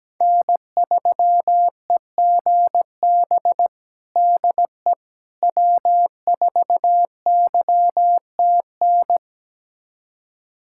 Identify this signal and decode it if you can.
Morse code, heard as a keyed tone, about 17 wpm, 700 Hz. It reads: N3EGB DE W4YTN
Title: USA Callsign DE Callsign - 5 Characters - 17wpm